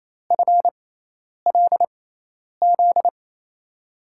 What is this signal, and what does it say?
Morse code, heard as a keyed tone, about 28 wpm, spaced as for 10 wpm, 700 Hz. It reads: FLZ
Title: USA Call Signs - 6 Characters - 6x Character Spacing - 28wpm